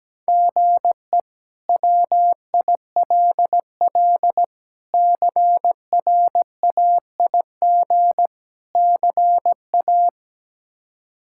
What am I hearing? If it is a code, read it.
Morse code, heard as a keyed tone, about 17 wpm, 700 Hz. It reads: GE WILL CRAIG CA
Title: K1USN SST QSO Practice - 17wpm